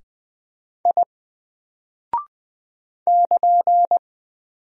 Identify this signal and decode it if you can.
Morse code, heard as a keyed tone, about 20 wpm, 700 Hz. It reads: I <KN>
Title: Single Character (Letter:Number:Punctuation:Prosign) Alt 20wpm